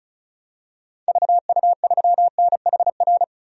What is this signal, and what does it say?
Morse code, heard as a keyed tone, about 35 wpm, 700 Hz. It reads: VU3NHR
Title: DXCC-100 Call Signs - 6 Characters - 35wpm